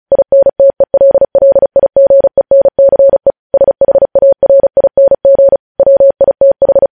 Morse code, heard as a keyed tone, about 36 wpm, 550 Hz. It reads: INTELLIGENCE SHARING WITH
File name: podcast / News headlines-1x-35x35